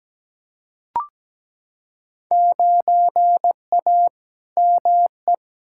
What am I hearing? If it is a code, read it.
Morse code, heard as a keyed tone, about 17 wpm, 700 Hz. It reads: 9A ME
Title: Field Day Exchanges Encore - 17wpm